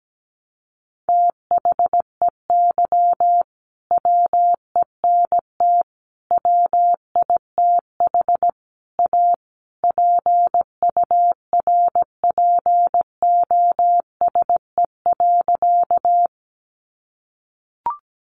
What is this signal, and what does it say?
Morse code, heard as a keyed tone, about 17 wpm, 700 Hz. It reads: THEY WENT WITH A PURPOSE.
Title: Sentences from Top 800 Words 17wpm